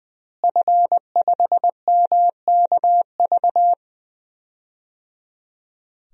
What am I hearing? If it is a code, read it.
Morse code, heard as a keyed tone, about 20 wpm, 700 Hz. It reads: F5MKV
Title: DXCC-100 Call Signs - 5 Characters - Repeated 3x - 20wpm